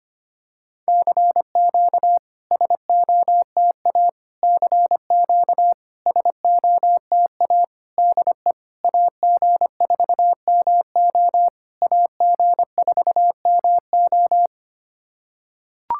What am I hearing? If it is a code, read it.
Morse code, heard as a keyed tone, about 25 wpm, 690 Hz. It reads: CQ SOTA CQ SOTA DE AG4MO AG4MO
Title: SOTA QSO Practice 25wpm